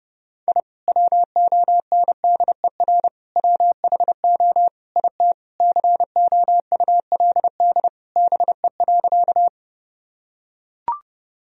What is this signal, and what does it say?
Morse code, heard as a keyed tone, about 30 wpm, 700 Hz. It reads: I WONDER WHO IT COULD BE.
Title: Sentences from Top 1000 Words - Review Part 3 - 30wpm